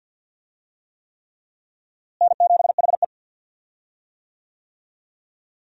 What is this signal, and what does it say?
Morse code, heard as a keyed tone, about 52 wpm, 700 Hz. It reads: N7HE
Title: USA Call Signs - 4 Characters - Repeated 3x - 50wpm